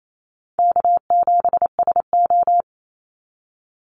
Morse code, heard as a keyed tone, about 28 wpm, 700 Hz. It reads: K7SO
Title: NA SOTA Activator Callsigns 28wpm